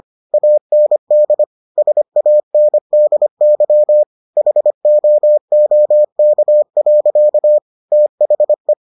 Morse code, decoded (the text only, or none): AND SANDY HOOK. THE